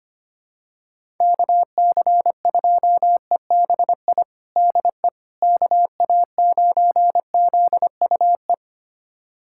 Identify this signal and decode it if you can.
Morse code, heard as a keyed tone, about 25 wpm, 700 Hz. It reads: KC2EBI DE KA9ZUE
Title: USA Callsign DE Callsign 25wpm